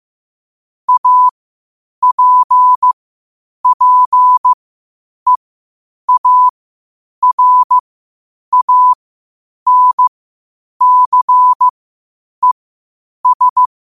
Morse code, heard as a keyed tone, about 15 wpm, spaced as for 8 wpm, 1000 Hz. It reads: APPEARANCES
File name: qotdfiles / 8wpm